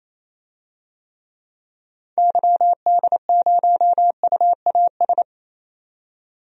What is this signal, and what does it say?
Morse code, heard as a keyed tone, about 28 wpm, 700 Hz. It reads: YD0UAS
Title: DXCC-100 Call Signs - 6 Characters - 28wpm